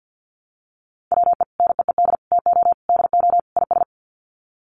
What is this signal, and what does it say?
Morse code, heard as a keyed tone, about 50 wpm, 700 Hz. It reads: WE NEED TO DO IS
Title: Sets of 5 Words Encore2 50wpm